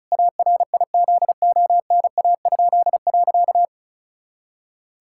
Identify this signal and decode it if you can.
Morse code, heard as a keyed tone, about 36 wpm, 700 Hz. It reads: ARIZONA?.
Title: Pangram Challenge v2 - 35wpm